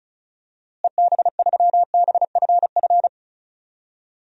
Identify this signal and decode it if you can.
Morse code, heard as a keyed tone, about 36 wpm, 700 Hz. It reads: EB3BFF
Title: DXCC-100 Call Signs - 6 Characters - Repeated 3x - 35wpm